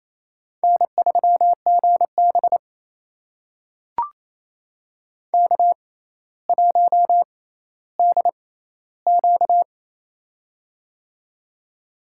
Morse code, heard as a keyed tone, about 28 wpm, spaced as for 14 wpm, 700 Hz. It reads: N3GB K1DQ
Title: USA Call Signs - 4 Characters - 6x Character Spacing - 28wpm